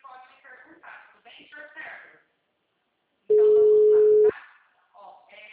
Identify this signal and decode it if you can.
Ringback tone (Europe, 425 Hz), with speech.